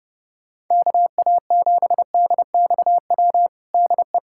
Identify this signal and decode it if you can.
Morse code, heard as a keyed tone, about 30 wpm, 700 Hz. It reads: KA7DXW DE
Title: USA Callsign DE Callsign 30wpm